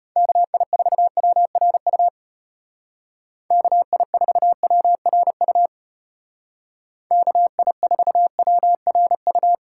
Morse code, heard as a keyed tone, about 32 wpm, 700 Hz. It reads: KI4WRU KI4WRU KI4WRU